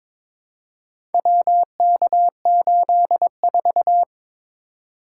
Morse code, heard as a keyed tone, about 22 wpm, 700 Hz. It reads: WK84
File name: practice / 22wpm / Four Letter-Number Combinations - 22wpm